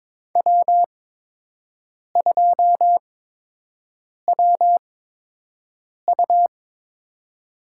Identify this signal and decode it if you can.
Morse code, heard as a keyed tone, about 22 wpm, spaced as for 6 wpm, 700 Hz. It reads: W2WU
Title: USA Call Signs - 8x Character Spacing - 22wpm